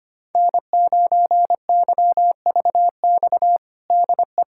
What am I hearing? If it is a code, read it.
Morse code, heard as a keyed tone, about 25 wpm, 700 Hz. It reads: N9YVX DE
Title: USA Callsign DE Callsign - 5 Characters - 25wpm